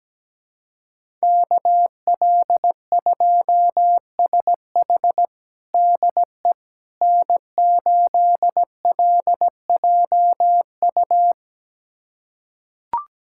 Morse code, heard as a keyed tone, about 17 wpm, 700 Hz. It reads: KL2SH DE N8LJU
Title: USA Callsign DE Callsign 17wpm